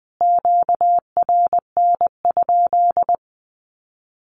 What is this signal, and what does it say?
Morse code, heard as a keyed tone, about 19 wpm, 700 Hz. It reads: QRN?
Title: Q-Codes Repeated 3x 20wpm